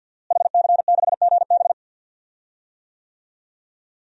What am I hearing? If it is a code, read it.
Morse code, heard as a keyed tone, about 47 wpm, 700 Hz. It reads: HC6GB
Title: DXCC-100 Call Signs - 5 Characters - Repeated 2x - 50wpm